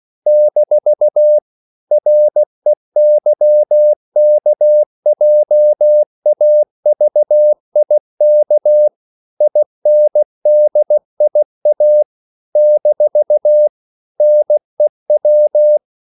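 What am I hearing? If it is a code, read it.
Morse code, heard as a keyed tone, about 16 wpm, 600 Hz. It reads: - REYKJAVIK INDIA - NEW
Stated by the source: CW practice of national capitals at 16 words per minute, from ON6ZQ.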